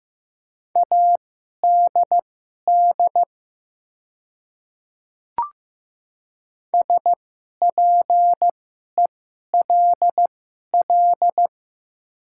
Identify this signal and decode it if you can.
Morse code, heard as a keyed tone, about 15 wpm, spaced as for 11 wpm, 700 Hz. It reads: ADD SPELL
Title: 2x Character Spacing - Top 200 Words - 15wpm